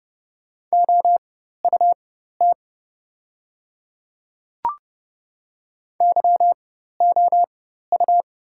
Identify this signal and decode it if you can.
Morse code, heard as a keyed tone, about 30 wpm, spaced as for 14 wpm, 700 Hz. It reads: OUT YOU
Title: 4x Character Spacing - Top 100 Words - 30wpm